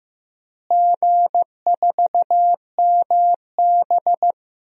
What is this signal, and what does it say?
Morse code, heard as a keyed tone, about 15 wpm, 700 Hz. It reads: G4MB
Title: DXCC-100 Call Signs - 4 Characters - Repeated 3x - 15wpm